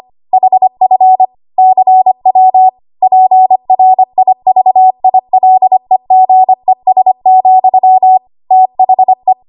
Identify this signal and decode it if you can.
Morse code, heard as a keyed tone, about 25 wpm, 750 Hz. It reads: HF CW PRIVILEGES, THE
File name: Archive / 25 WPM / 25WPM